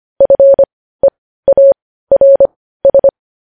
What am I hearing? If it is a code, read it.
Morse code, heard as a keyed tone, about 25 wpm, spaced as for 15 wpm, 550 Hz. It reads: FEARS
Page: news headlines in morse code at 15 wpm